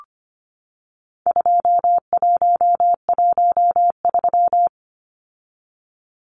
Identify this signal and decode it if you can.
Morse code, heard as a keyed tone, about 25 wpm, 700 Hz. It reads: 2113
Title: Numbers - 4-Digit 25wpm